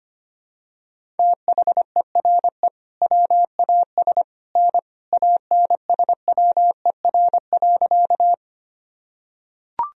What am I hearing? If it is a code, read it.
Morse code, heard as a keyed tone, about 25 wpm, 700 Hz. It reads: THERE WAS N ANSWER.
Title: Sentences from Top 300 Words and Abbreviations 25wpm